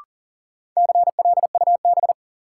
Morse code, heard as a keyed tone, about 40 wpm, 700 Hz. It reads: CLUB
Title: Sentences from Top 1200 Words - New Word Drill Alt - 40wpm